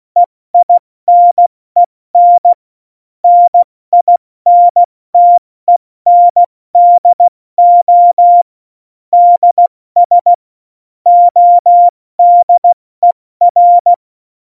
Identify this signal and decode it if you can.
Morse code, heard as a keyed tone, about 16 wpm, 700 Hz. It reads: EINEN NINTENDO DS ODER
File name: Morsekurs / Witze / Witz 15. Januar 2019, 70 F80